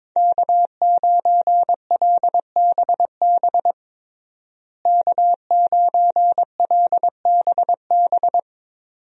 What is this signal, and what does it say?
Morse code, heard as a keyed tone, about 22 wpm, 700 Hz. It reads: K9LBB K9LBB
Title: USA Call Signs - 5 Characters - Repeated 2x - 22wpm